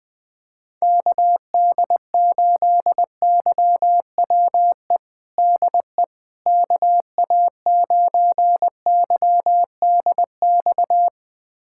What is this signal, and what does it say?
Morse code, heard as a keyed tone, about 21 wpm, 700 Hz. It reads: KD8YWE DE KA9YDX
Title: USA Callsign DE Callsign 20wpm